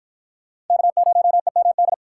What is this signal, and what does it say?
Morse code, heard as a keyed tone, about 44 wpm, 700 Hz. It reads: K0EMD